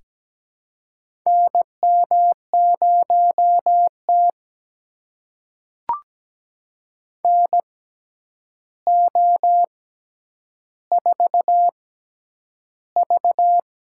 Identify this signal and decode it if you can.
Morse code, heard as a keyed tone, about 17 wpm, spaced as for 9 wpm, 700 Hz. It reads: NM0T NO4V
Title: USA Call Signs - 4 Characters - 6x Character Spacing - 17wpm